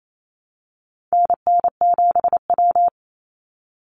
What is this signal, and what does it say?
Morse code, heard as a keyed tone, about 28 wpm, 700 Hz. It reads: NN7W
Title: Call Signs - 28wpm